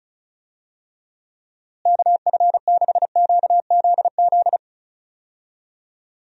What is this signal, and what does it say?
Morse code, heard as a keyed tone, about 36 wpm, 700 Hz. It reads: KF6QZZ
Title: USA Call Signs - 6 Characters - Repeated 2x - 35wpm